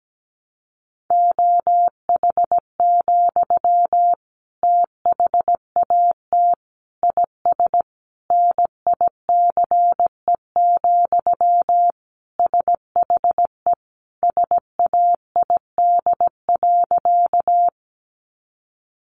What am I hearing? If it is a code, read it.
Morse code, heard as a keyed tone, about 17 wpm, 700 Hz. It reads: OH, THAT IS NICE, SHE SAID.
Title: Sentences from Top 1100 Words 17wpm